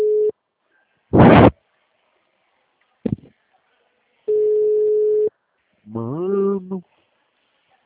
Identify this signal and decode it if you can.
Ringback tone (Europe, 425 Hz), with speech.